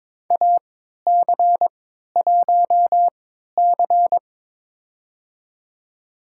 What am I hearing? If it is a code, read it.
Morse code, heard as a keyed tone, about 22 wpm, spaced as for 12 wpm, 700 Hz. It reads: AC1C
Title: USA Call Signs - 3x Character Spacing - 22wpm